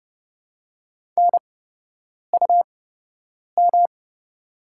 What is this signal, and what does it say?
Morse code, heard as a keyed tone, about 30 wpm, spaced as for 8 wpm, 700 Hz. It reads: NUM
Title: 8x Character Spacing - Top 100 Words - 30wpm